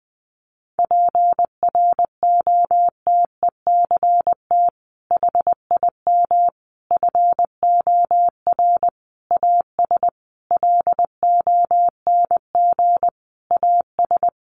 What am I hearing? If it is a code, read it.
Morse code, heard as a keyed tone, about 20 wpm, 700 Hz. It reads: PROTECT HIM FOR AS LONG AS
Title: Sentences from Top 900 Words 20wpm